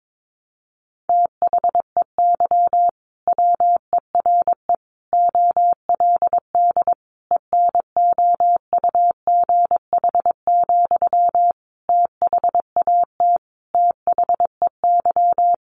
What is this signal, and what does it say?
Morse code, heard as a keyed tone, about 22 wpm, 700 Hz. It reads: THEY WERE OLD ENOUGH, THAT THEY
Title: Sentences from Top 800 Words 22wpm